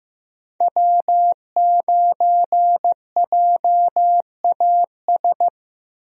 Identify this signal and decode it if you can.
Morse code, heard as a keyed tone, about 15 wpm, 700 Hz. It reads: W9JAS
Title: USA Call Signs - 5 Characters - Repeated 3x - 15wpm